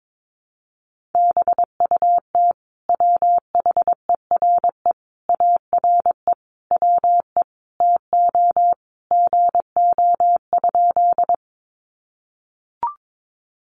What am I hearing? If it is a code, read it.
Morse code, heard as a keyed tone, about 22 wpm, 700 Hz. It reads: BUT WHERE ARE WE TO GO?
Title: Sentences from Top 200 Words 22wpm